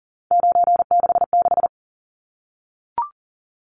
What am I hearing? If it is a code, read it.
Morse code, heard as a keyed tone, about 40 wpm, 700 Hz. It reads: 966
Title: Numbers - 3-Digit 40wpm